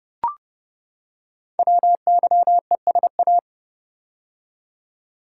Morse code, heard as a keyed tone, about 30 wpm, 700 Hz. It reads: WYESA
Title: Code Groups - 5 Letters - 30wpm